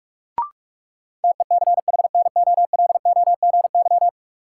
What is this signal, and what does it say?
Morse code, heard as a keyed tone, about 46 wpm, 700 Hz. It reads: TECHNOLOGY
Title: Sentences from Top 1200 Words - New Word Drill Alt - 45wpm